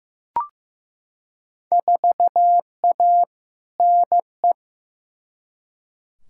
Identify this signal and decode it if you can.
Morse code, heard as a keyed tone, about 15 wpm, 700 Hz. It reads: 4A NE
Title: Field Day Exchanges - 15wpm